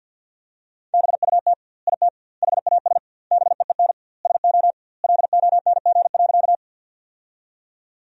Morse code, heard as a keyed tone, about 48 wpm, 700 Hz. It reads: BUT IT HAS BEEN SO LONG.
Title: 2x Word-Spacing - Sentences from Top 100 Words - 50wpm